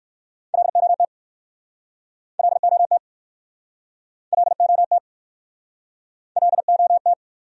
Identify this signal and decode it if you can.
Morse code, heard as a keyed tone, about 58 wpm, 690 Hz. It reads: LOT LOT LOT LOT